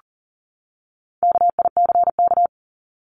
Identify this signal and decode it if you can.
Morse code, heard as a keyed tone, about 40 wpm, 700 Hz. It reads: KICK
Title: Top 1251-1500 Words 40wpm